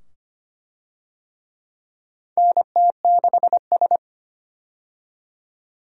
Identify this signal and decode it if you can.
Morse code, heard as a keyed tone, about 25 wpm, 700 Hz. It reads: NT6S
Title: USA Call Signs - 4 Characters - 25wpm